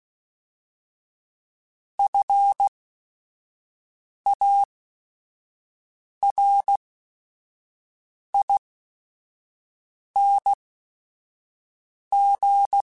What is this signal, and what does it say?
Morse code, heard as a keyed tone, about 16 wpm, spaced as for 5 wpm, 800 Hz. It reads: FARING
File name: Special INGs / Special INGs-mp3s / Sngl Ing @5